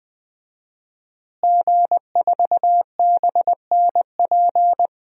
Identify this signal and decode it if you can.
Morse code, heard as a keyed tone, about 20 wpm, 700 Hz. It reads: G4BNP
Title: DXCC-100 Call Signs - 5 Characters - Repeated 3x - 20wpm